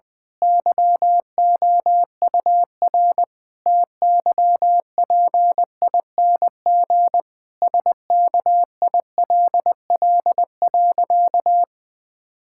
Morse code, heard as a keyed tone, about 20 wpm, 700 Hz. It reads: YOUR TYPING SKILL.
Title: Pangram Challenge v2 - 20wpm